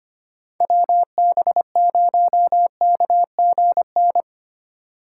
Morse code, heard as a keyed tone, about 31 wpm, 700 Hz. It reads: WB0KGN
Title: Speed Racing - USA Call Signs 25wpm